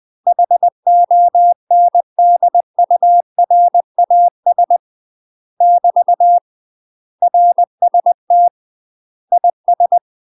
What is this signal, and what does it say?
Morse code, heard as a keyed tone, about 20 wpm, 700 Hz. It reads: HONDURAS <BT> RST IS
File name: UK MASTER QSOs TRAINING / 20WPM / UK MASTER QSOs 0071